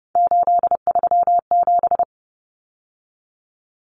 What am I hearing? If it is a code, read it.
Morse code, heard as a keyed tone, about 30 wpm, 700 Hz. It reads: 837
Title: Numbers - 3-Digit 30wpm